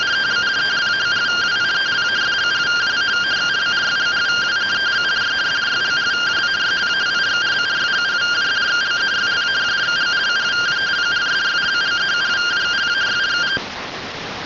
Two-tone teleprinter signal, about 170 Hz shift.